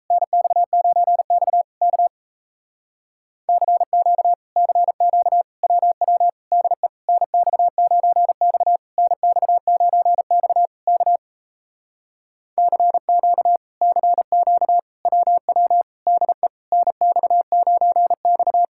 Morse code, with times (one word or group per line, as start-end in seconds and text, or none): 0.1-1.6 NX9X
1.8-2.1 K
3.5-4.3 CQ
4.6-5.4 CQ
5.6-6.3 WW
6.5-6.9 DE
7.1-8.8 NX9X
9.0-10.7 NX9X
10.9-11.2 K
12.6-13.6 CQ
13.8-14.8 CQ
15.1-15.8 WW
16.1-16.5 DE
16.7-18.6 NX9X